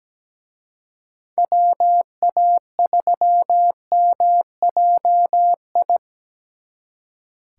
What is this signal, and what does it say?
Morse code, heard as a keyed tone, about 20 wpm, 700 Hz. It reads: WA3MJI
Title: Speed Racing - USA 2x3 Call Signs 17wpm